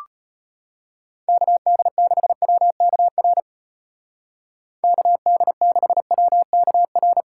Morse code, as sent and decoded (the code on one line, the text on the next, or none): -.- -.. -.... .-- -.- .-. / -.- -.. -.... .-- -.- .-.
KD6WKR KD6WKR